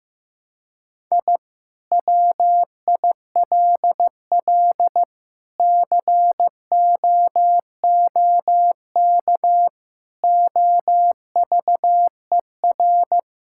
Morse code, heard as a keyed tone, about 15 wpm, 700 Hz. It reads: I WILL COOK OVER
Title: Sentences from Top 1000 Words - Review Part 6 - 15wpm